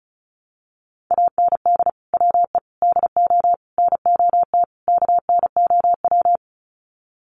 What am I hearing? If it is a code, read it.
Morse code, heard as a keyed tone, about 34 wpm, 700 Hz. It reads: AND WE DO NOT KNOW
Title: Sets of 5 Words Encore 35wpm